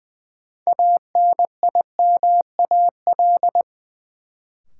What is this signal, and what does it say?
Morse code, heard as a keyed tone, about 20 wpm, 700 Hz. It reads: ANIMAL
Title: Sentences from Top 1000 Words - Review Part 1 - New Word Drill - 20wpm